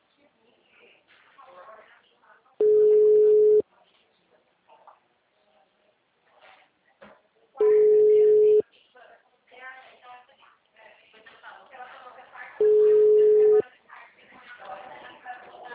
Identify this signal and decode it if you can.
Ringback tone (Europe, 425 Hz), with speech.